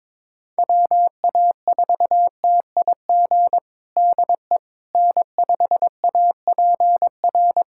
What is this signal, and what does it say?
Morse code, heard as a keyed tone, about 22 wpm, 700 Hz. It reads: WA4TIG DE N5APR
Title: USA Callsign DE Callsign 22wpm